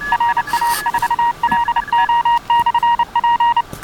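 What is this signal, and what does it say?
Morse code, heard as a keyed tone, about 29 wpm, 940 Hz. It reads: RA4LO/P
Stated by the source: Portable QRP station operating in a Field Day contest.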